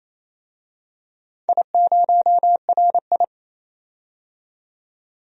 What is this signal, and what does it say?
Morse code, heard as a keyed tone, about 27 wpm, 690 Hz. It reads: I0RI
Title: DXCC-100 Call Signs - 4 Characters - 28wpm